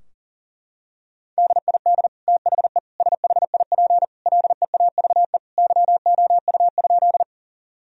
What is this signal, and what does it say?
Morse code, heard as a keyed tone, about 40 wpm, 700 Hz. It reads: DID THE SHIP LEAVE YOU?
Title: Sentences from Top 1000 Words - Review Part 2 - 40wpm